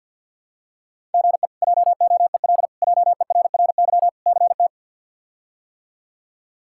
Morse code, heard as a keyed tone, about 49 wpm, 700 Hz. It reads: GE JOEL JERRY CT
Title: K1USN SST QSO Practice - 50wpm